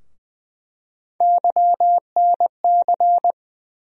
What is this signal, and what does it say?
Morse code, heard as a keyed tone, about 20 wpm, 700 Hz. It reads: YNC
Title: USA Call Sign Postfixes - 3 Letters 20wpm